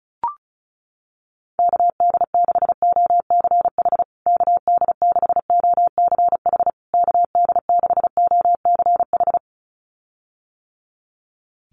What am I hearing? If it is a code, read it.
Morse code, heard as a keyed tone, about 34 wpm, 700 Hz. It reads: KD6OCH KD6OCH KD6OCH
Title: USA Call Signs Repeated 3x 35wpm